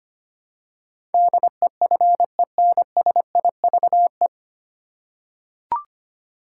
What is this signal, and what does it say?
Morse code, heard as a keyed tone, about 25 wpm, 700 Hz. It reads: DEFENSIVE
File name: practice / 25wpm / 9 Letter Words - First 3 Letters Given - 25wpm